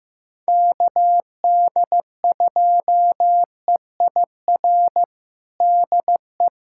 Morse code, heard as a keyed tone, about 16 wpm, 700 Hz. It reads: KD2EIR DE
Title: USA Callsign DE Callsign - 6 Characters - 15wpm